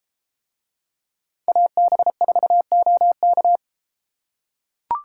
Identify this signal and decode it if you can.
Morse code, heard as a keyed tone, about 34 wpm, 700 Hz. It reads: AB4OK